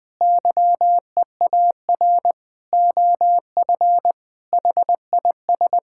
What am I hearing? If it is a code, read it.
Morse code, heard as a keyed tone, about 21 wpm, 700 Hz. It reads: YEAR OF HIS
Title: Sets of 6 Words - 20wpm